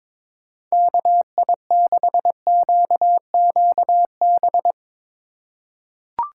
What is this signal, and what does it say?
Morse code, heard as a keyed tone, about 22 wpm, 700 Hz. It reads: KI6QQB
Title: USA Call Signs - 6 Characters - 2x Character Spacing - 22wpm